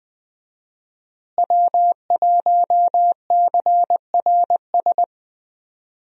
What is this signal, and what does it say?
Morse code, heard as a keyed tone, about 20 wpm, 700 Hz. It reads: W1CRS
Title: USA Call Signs - 5 Characters - Repeated 3x - 20wpm